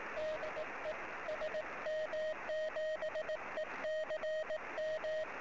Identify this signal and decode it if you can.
Morse code, heard as a keyed tone, about 18 wpm, 620 Hz, with radio noise. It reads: DE SM7ECM
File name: sm7lcb / 11 sm7ecm 23cm